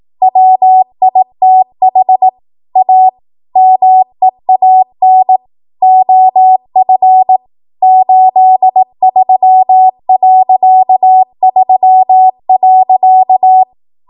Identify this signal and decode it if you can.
Morse code, heard as a keyed tone, about 18 wpm, 750 Hz. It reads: WITH A MEAN OF 83.3.
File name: Archive / 18 WPM / 18WPM